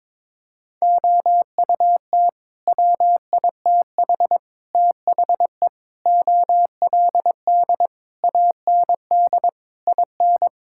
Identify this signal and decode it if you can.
Morse code, heard as a keyed tone, about 22 wpm, 700 Hz. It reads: OUT WITH THE OLD AND IN
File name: Sets of 9 Words - 22wpm